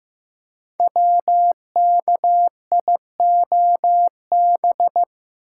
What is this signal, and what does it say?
Morse code, heard as a keyed tone, about 15 wpm, 700 Hz. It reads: WKIOB
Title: Code Groups - 5 Letters - 15wpm